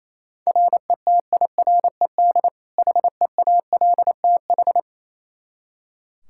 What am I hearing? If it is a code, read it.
Morse code, heard as a keyed tone, about 28 wpm, 700 Hz. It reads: RETIRED HEALTH
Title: Practice QSO - Exchange 3 - More Info - 28wpm